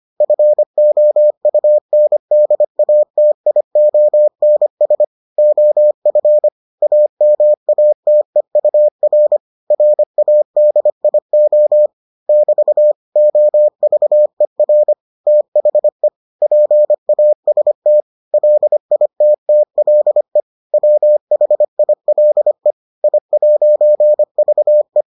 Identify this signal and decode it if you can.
Morse code, heard as a keyed tone, about 25 wpm, 600 Hz. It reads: FOUNDATIONS OF AMATEUR RADIO <BT> OVER THE PAST LITTLE WHILE I'VE